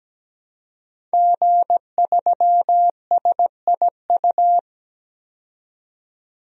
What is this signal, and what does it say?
Morse code, heard as a keyed tone, about 17 wpm, 700 Hz. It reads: G3SIU
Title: DXCC-100 Call Signs - 5 Characters - 17wpm